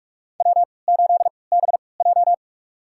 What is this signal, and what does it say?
Morse code, heard as a keyed tone, about 44 wpm, spaced as for 25 wpm, 700 Hz. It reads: W8BJ
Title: USA Call Signs - 4 Characters - 3x Character Spacing - 45wpm